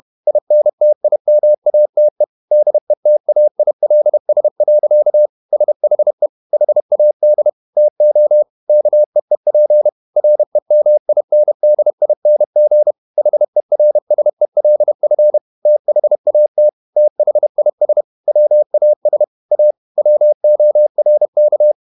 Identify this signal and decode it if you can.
Morse code, heard as a keyed tone, about 31 wpm, 600 Hz. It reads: INTIMATE DETAILS. SHE HAD TO KEEP REMINDING HERSELF THAT THIS WAS A WORK
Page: Never 31wpm | CW med Gnister